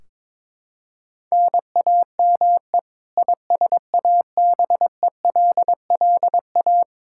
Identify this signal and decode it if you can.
Morse code, heard as a keyed tone, about 22 wpm, 700 Hz. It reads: NAME ISABELLA
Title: Practice QSO - Exchange 1 - Beginning - 22wpm